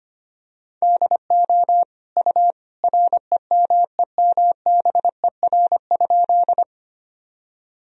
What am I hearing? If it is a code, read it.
Morse code, heard as a keyed tone, about 25 wpm, 700 Hz. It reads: DO U REMEMBER?
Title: Sentences from Top 400 Words and Abbreviations 25wpm